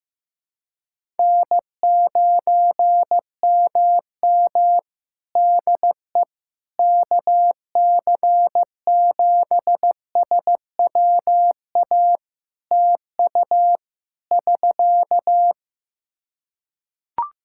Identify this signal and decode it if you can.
Morse code, heard as a keyed tone, about 15 wpm, 700 Hz. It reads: N9MM DE KC7SWA TU <SK>
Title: Practice QSO - Exchange 4 - Ending - 15wpm